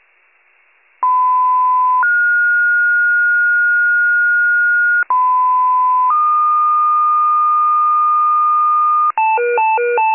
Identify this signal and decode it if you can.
Two-tone page, heard as a one-second tone, then a three-second tone, about 1010 then 1430 Hz, about 1010 then 1220 Hz, with noise. It is heard from 1.0 to 9.1 s.